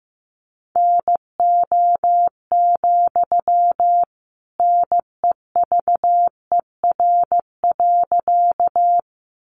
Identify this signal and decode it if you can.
Morse code, heard as a keyed tone, about 15 wpm, 700 Hz. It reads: NO, NEVER.
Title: Sentences from Top 300 Words 15wpm